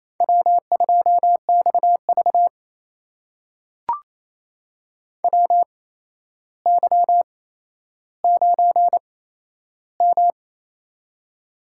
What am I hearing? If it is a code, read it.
Morse code, heard as a keyed tone, about 28 wpm, spaced as for 12 wpm, 700 Hz. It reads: W2XV WY9M
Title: USA Call Signs - 4 Characters - 8x Character Spacing - 28wpm